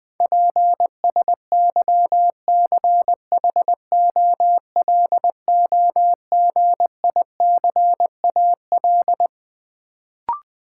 Morse code, heard as a keyed tone, about 20 wpm, 700 Hz. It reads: PSYCHOLOGICAL
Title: Sentences from Top 1400 Words - New Word Drill - 20wpm